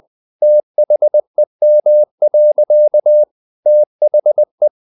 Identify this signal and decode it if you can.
Morse code, heard as a keyed tone, about 20 wpm, 600 Hz. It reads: THEM. THE
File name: the old man and the sea / the old man and the sea part 1 20wpm